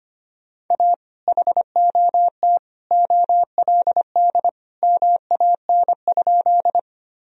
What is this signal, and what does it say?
Morse code, heard as a keyed tone, about 25 wpm, 700 Hz. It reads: A HOT OLD MAN?
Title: Sentences from Top 1000 Words - Review Part 3 - 25wpm